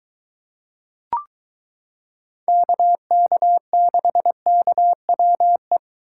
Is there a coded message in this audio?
Yes: Morse code from 2.5 to 5.8 s, heard as a keyed tone, about 23 wpm, 700 Hz. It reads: KK6KWE